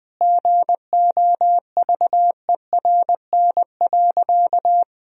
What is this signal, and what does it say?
Morse code, heard as a keyed tone, about 21 wpm, 700 Hz. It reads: GOVERN.
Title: Sentences from Top 1000 Words - Review Part 2 - 20wpm